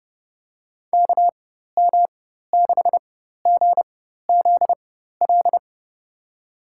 Morse code, heard as a keyed tone, about 30 wpm, spaced as for 14 wpm, 700 Hz. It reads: KM6GZL
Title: USA Call Signs - 6 Characters - 4x Character Spacing - 30wpm